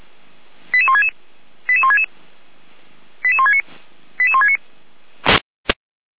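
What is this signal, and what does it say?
Five-tone selective call (ZVEI), heard as quick runs of five tones, tones of about 70 ms. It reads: 88166 88166 88169 88169
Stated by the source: ZVEI five-tone selective call received on SDR.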